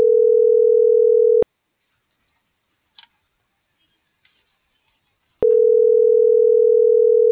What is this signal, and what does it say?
Ringback tone (North America), 440 + 480 Hz.